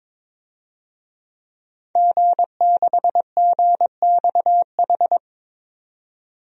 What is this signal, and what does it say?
Morse code, heard as a keyed tone, about 22 wpm, 700 Hz. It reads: G6GXH
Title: DXCC-100 Call Signs - 5 Characters - 22wpm